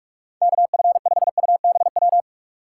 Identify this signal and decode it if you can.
Morse code, heard as a keyed tone, about 44 wpm, 700 Hz. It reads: KF5UBW